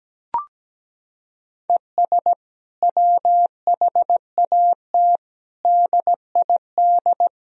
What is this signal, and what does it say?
Morse code, heard as a keyed tone, about 17 wpm, 700 Hz. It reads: ES WHAT DID
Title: Sentences from Top 200 Words and Abbreviations 17wpm